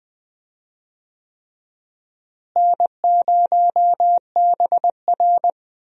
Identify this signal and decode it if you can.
Morse code, heard as a keyed tone, about 20 wpm, 700 Hz. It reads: N0BR
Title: USA Call Signs - 4 Characters - Repeated 3x - 20wpm